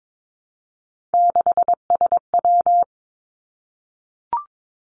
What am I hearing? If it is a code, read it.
Morse code, heard as a keyed tone, about 22 wpm, 700 Hz. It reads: 6SW
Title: Three Challenging Letter-Number Combinations 22wpm